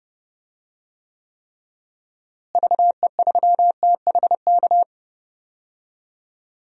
Morse code, heard as a keyed tone, about 30 wpm, 700 Hz. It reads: VE3THK
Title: DXCC-100 Call Signs - 6 Characters - Repeated 3x - 30wpm